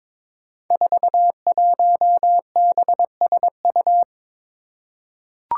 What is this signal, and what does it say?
Morse code, heard as a keyed tone, about 22 wpm, 700 Hz. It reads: 41BSU
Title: Five Challenging Letter-Number Combinations 22wpm